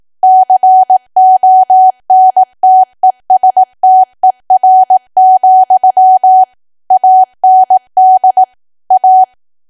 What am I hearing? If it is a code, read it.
Morse code, heard as a keyed tone, about 18 wpm, 750 Hz. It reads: CONTESTER, AND A
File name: Archive / 18 WPM / 18WPM